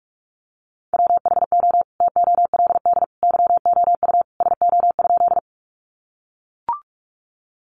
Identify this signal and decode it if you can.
Morse code, heard as a keyed tone, about 44 wpm, 700 Hz. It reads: WHO TOLD YOU SO?
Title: Sentences from Top 400 Words 45wpm